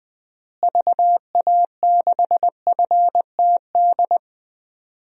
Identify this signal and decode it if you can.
Morse code, heard as a keyed tone, about 20 wpm, 700 Hz. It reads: VA6FTD
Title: DXCC-100 Call Signs - 6 Characters - Repeated 3x - 20wpm